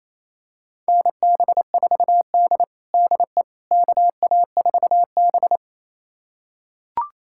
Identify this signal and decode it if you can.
Morse code, heard as a keyed tone, about 27 wpm, 700 Hz. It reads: NB4D DE KA4B
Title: USA Callsign DE Callsign - 4 Characters - 28wpm